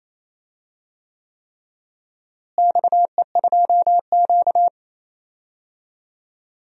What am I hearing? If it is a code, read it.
Morse code, heard as a keyed tone, about 28 wpm, 700 Hz. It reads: XE2Q
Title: DXCC-100 Call Signs - 4 Characters - Repeated 3x - 28wpm